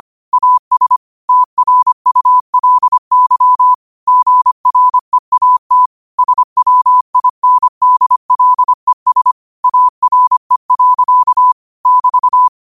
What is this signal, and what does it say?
Morse code, heard as a keyed tone, about 25 wpm, 1000 Hz. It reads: AS TRULY GREAT SWINDLES ARE. <BT>
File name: qotdfiles / 25wpm